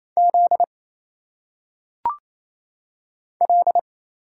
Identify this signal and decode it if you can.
Morse code, heard as a keyed tone, about 27 wpm, 700 Hz. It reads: Z L
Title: Single Letters - Phonetic Pronunciation 28wpm